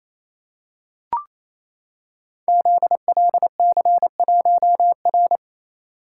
Morse code, heard as a keyed tone, about 27 wpm, 700 Hz. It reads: ZLC1R
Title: Code Groups - 5 Letter:Numbers - 28wpm